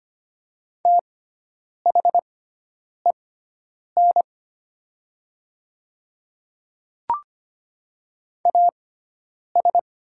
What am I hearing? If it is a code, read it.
Morse code, heard as a keyed tone, about 25 wpm, spaced as for 9 wpm, 700 Hz. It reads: THEN AS
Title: 6x Character Spacing - Top 100 Words - 25wpm